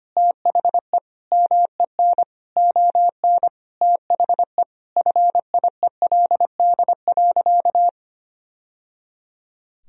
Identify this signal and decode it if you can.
Morse code, heard as a keyed tone, about 25 wpm, 700 Hz. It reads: THE MEN ON THE FIELD.
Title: Sentences from Top 1000 Words - Review Part 2 - 25wpm